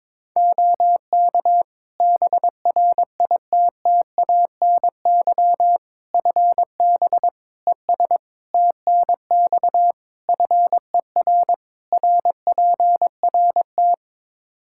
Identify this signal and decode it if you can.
Morse code, heard as a keyed tone, about 21 wpm, 700 Hz. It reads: OK BRITTANY FB ES TNX FER RPRT
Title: K1USN SST QSO Practice 22wpm